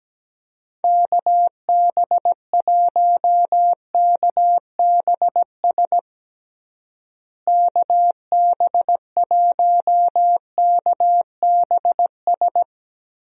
Morse code, heard as a keyed tone, about 17 wpm, 700 Hz. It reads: KB1KBS KB1KBS
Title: USA Call Signs - 6 Characters - Repeated 3x - 17wpm